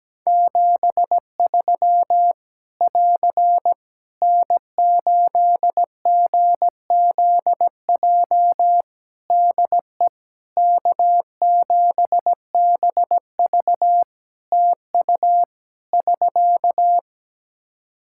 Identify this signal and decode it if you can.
Morse code, heard as a keyed tone, about 17 wpm, 700 Hz. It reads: 73 <AR> N8GZJ DE K7BV TU <SK>
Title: Practice QSO - Exchange 4 - Ending - 17wpm